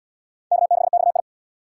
Morse code, heard as a keyed tone, about 72 wpm, 700 Hz. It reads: K6XI